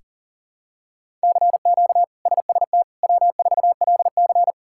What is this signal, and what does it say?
Morse code, heard as a keyed tone, about 40 wpm, 700 Hz. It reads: CQ SST W4LC
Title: K1USN SST QSO Practice - 40wpm